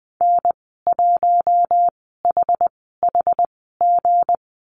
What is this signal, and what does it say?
Morse code, heard as a keyed tone, about 20 wpm, spaced as for 15 wpm, 700 Hz. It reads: N1HHG
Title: 173 Lesson 28g - Number 9 - Call Signs 20wpm